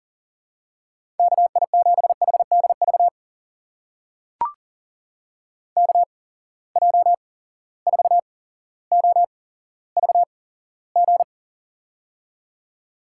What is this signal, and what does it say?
Morse code, heard as a keyed tone, about 40 wpm, spaced as for 17 wpm, 700 Hz. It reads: KI7HDV KJ4OVG
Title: USA Call Signs - 6 Characters - 8x Character Spacing - 40wpm